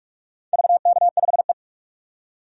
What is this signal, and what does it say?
Morse code, heard as a keyed tone, about 46 wpm, 700 Hz. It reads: VK5E
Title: DXCC-100 Call Signs - 4 Characters - 45wpm